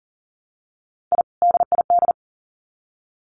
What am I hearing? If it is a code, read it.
Morse code, heard as a keyed tone, about 40 wpm, 700 Hz. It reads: I DID
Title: Sets of 2 Words 40wpm